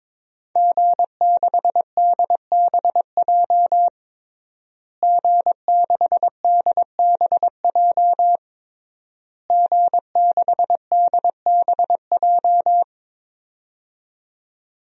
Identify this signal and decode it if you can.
Morse code, heard as a keyed tone, about 22 wpm, 700 Hz. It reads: G6DBJ G6DBJ G6DBJ
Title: DXCC-100 Call Signs - 5 Characters - Repeated 3x - 22wpm